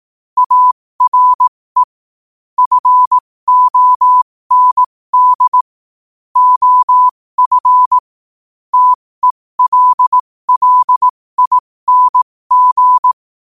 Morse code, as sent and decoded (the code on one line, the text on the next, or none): .- .-. / ..-. --- -. -.. / --- ..-. / - . .-.. .-.. .. -. --.
ARE FOND OF TELLING